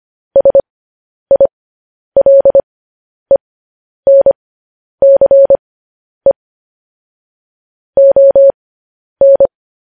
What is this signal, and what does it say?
Morse code, heard as a keyed tone, about 25 wpm, spaced as for 10 wpm, 550 Hz. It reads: SILENCE ON
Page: News Headlines in Morse Code at 10 WPM